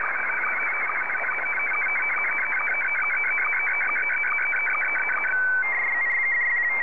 SSTV (slow-scan television), Scottie 1.